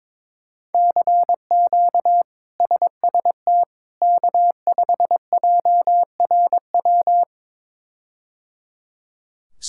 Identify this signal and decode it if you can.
Morse code, heard as a keyed tone, about 22 wpm, 700 Hz. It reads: CQ SST K5JRW
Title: K1USN SST QSO Practice - 22wpm